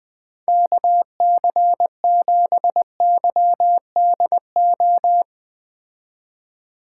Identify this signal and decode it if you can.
Morse code, heard as a keyed tone, about 20 wpm, 700 Hz. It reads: KC7YDO